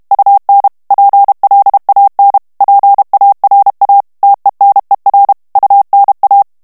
Morse code, heard as a keyed tone, about 32 wpm, 800 Hz. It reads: UN PLAN PARA TENER UNA
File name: morse mp3 / ESPAGNOL TEXTE EN CLAIR 1 VITESSE 30 MOTS MINUTES